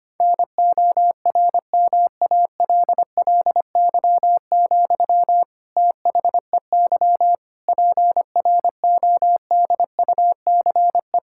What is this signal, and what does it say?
Morse code, heard as a keyed tone, about 25 wpm, 700 Hz. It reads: NORMALLY, THEY PRODUCE
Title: Sentences from Top 1000 Words - Review Part 6 - 25wpm